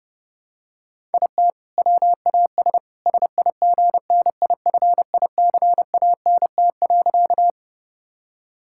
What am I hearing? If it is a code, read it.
Morse code, heard as a keyed tone, about 30 wpm, 700 Hz. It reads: IT WAS SIGNIFICANT.
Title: Sentences from Top 1000 Words - Review Part 6 - 30wpm